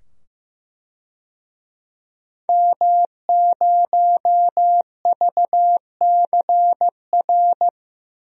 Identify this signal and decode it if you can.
Morse code, heard as a keyed tone, about 15 wpm, 700 Hz. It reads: M0VCR
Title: DXCC-100 Call Signs - 5 Characters - 15wpm